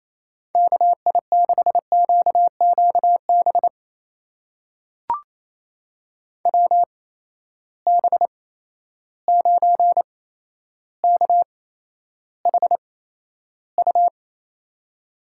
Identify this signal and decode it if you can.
Morse code, heard as a keyed tone, about 28 wpm, spaced as for 12 wpm, 700 Hz. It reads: KI6QQB WB9KHU
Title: USA Call Signs - 6 Characters - 8x Character Spacing - 28wpm